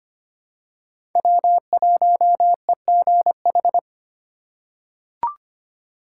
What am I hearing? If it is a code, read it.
Morse code, heard as a keyed tone, about 25 wpm, 700 Hz. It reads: W1EGH
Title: USA Call Signs - 5 Characters - 4x Character Spacing - 25wpm